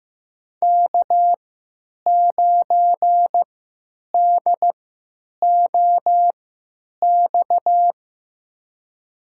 Morse code, heard as a keyed tone, about 15 wpm, spaced as for 9 wpm, 700 Hz. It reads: K9DOX
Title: USA Call Signs - 5 Characters - 3x Character Spacing - 15wpm